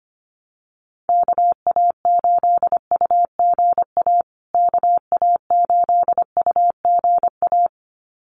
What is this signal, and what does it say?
Morse code, heard as a keyed tone, about 25 wpm, 700 Hz. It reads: KA8UGA KA8UGA
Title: USA Call Signs Repeated 2x 25wpm